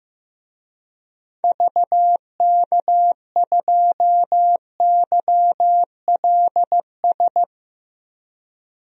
Morse code, heard as a keyed tone, about 15 wpm, 700 Hz. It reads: VK2YLS
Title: DXCC-100 Call Signs - 6 Characters - 15wpm